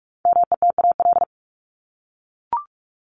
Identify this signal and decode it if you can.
Morse code, heard as a keyed tone, about 44 wpm, 700 Hz. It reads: METAL
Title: Top 751-1000 Words 45wpm